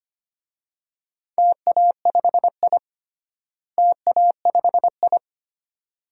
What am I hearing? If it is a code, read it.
Morse code, heard as a keyed tone, about 25 wpm, 700 Hz. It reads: TA5I TA5I
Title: DXCC-100 Call Signs - 4 Characters - Repeated 2x - 25wpm